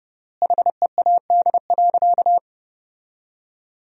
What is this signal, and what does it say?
Morse code, heard as a keyed tone, about 30 wpm, 700 Hz. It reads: HEAD.
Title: 3x Word-Spacing - Sentences from Top 300 Words - 30wpm